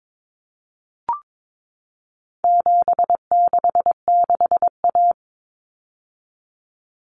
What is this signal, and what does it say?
Morse code, heard as a keyed tone, about 22 wpm, 700 Hz. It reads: 766A
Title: Common Cut Numbers - 4-Digit 22wpm